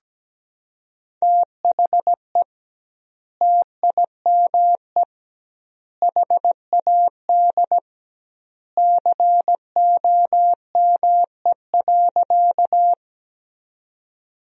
Morse code, heard as a keyed tone, about 17 wpm, 700 Hz. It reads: THE TIME HAD COME.
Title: 2x Word-Spacing - Sentences from Top 100 Words - 17wpm